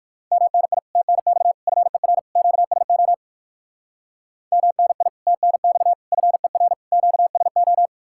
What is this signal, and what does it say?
Morse code, heard as a keyed tone, about 56 wpm, 700 Hz. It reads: MNI TNX FER QSO MNI TNX FER QSO
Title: Speed-Racing Common QSO Phrases - 45wpm